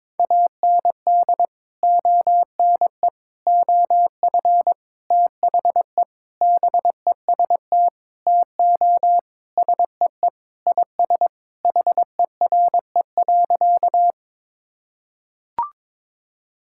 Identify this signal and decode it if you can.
Morse code, heard as a keyed tone, about 22 wpm, 700 Hz. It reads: AND ONE OF THE BEST TO SEE IS HERE.
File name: practice / 22wpm / Sentences with 4-Word Phrases - 22wpm